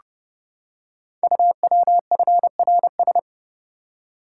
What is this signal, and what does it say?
Morse code, heard as a keyed tone, about 30 wpm, 690 Hz. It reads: UWFRS
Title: Code Groups - 5 Letters - 30wpm